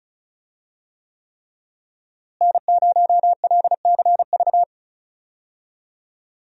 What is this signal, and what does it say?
Morse code, heard as a keyed tone, about 34 wpm, 700 Hz. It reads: N0LCV
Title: USA Call Signs - 5 Characters - Repeated 3x - 35wpm